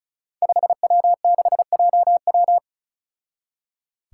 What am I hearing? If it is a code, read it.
Morse code, heard as a keyed tone, about 36 wpm, 700 Hz. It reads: 5W6JW
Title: Five Challenging Letter-Number Combinations 35wpm